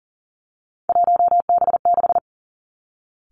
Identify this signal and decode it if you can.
Morse code, heard as a keyed tone, about 38 wpm, 690 Hz. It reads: 1B6
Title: Three Challenging Letter-Number Combinations 40wpm